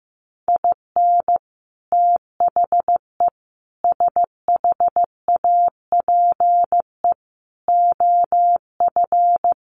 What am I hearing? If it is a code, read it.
Morse code, heard as a keyed tone, about 15 wpm, 700 Hz. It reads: IN THE SHAPE OF
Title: Sets of 5 Words Encore 15wpm